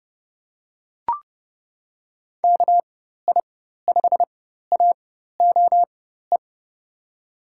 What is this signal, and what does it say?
Morse code, heard as a keyed tone, about 30 wpm, spaced as for 14 wpm, 700 Hz. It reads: KI5AOE
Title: USA Call Signs - 4x Character Spacing - 30wpm